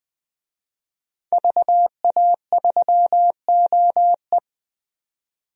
Morse code, heard as a keyed tone, about 20 wpm, 700 Hz. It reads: VA3OE
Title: DXCC-100 Call Signs - 5 Characters - 20wpm